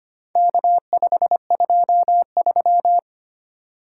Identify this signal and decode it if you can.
Morse code, heard as a keyed tone, about 25 wpm, 700 Hz. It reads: K523
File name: practice / 25wpm / Four Letter-Number Combinations - 25wpm